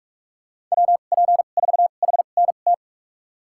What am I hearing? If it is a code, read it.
Morse code, heard as a keyed tone, about 46 wpm, spaced as for 33 wpm, 700 Hz. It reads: WP4HNT
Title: USA Call Signs - 6 Characters - 2x Character Spacing - 45wpm